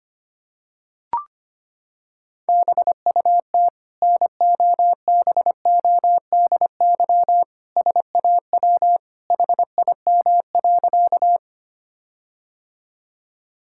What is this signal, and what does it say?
Morse code, heard as a keyed tone, about 25 wpm, 700 Hz. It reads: BUT NOBODY SAW HIM.
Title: Sentences from Top 1600 Words 25wpm